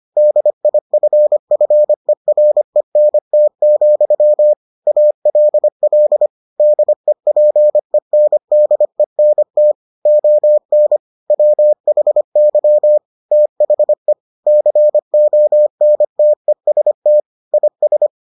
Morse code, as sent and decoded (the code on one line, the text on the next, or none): -.. .. ..-. ..-. .-. -. - --..-- / .- .-.. .-.. / -.. .--. -. -.. -. - / --- -. / .-- .... -.-- / - .... / -.-. --- -. - . ... - / .. ...
DIFFERENT, ALL DEPENDENT ON WHY THE CONTEST IS